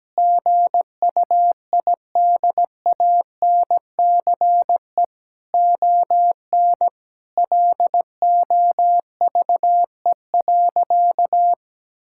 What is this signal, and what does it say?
Morse code, heard as a keyed tone, about 17 wpm, 700 Hz. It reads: GUIDANCE ON LOVE.
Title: Sentences from Top 1600 Words 17wpm